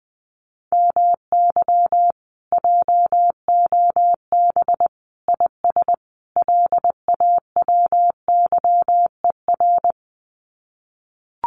Morse code, heard as a keyed tone, about 20 wpm, 700 Hz. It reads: MY JOB IS LAWYER
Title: Common QSO Phrases Encore 20wpm